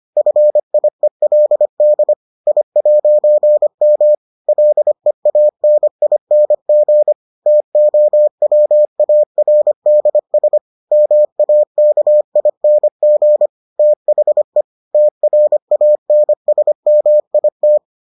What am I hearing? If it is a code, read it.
Morse code, heard as a keyed tone, about 25 wpm, 600 Hz. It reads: FIELD I'M LEANING TOWARDS MAKING THE TRANSMIT